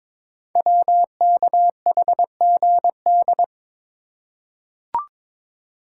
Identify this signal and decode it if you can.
Morse code, heard as a keyed tone, about 23 wpm, 700 Hz. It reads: WKHGD
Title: Code Groups - 5 Letters - 22wpm